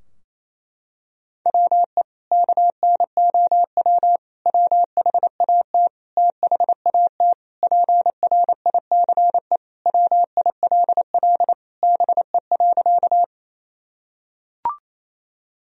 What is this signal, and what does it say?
Morse code, heard as a keyed tone, about 28 wpm, 700 Hz. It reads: WE KNOW WHAT THAT PRICE WILL BE.
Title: Sentences from Top 1000 Words - Review Part 4 - 28wpm